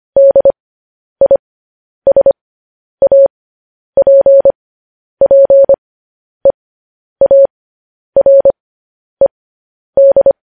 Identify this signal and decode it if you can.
Morse code, heard as a keyed tone, about 25 wpm, spaced as for 10 wpm, 550 Hz. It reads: DISAPPEARED
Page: news headlines in morse code at 10 wpm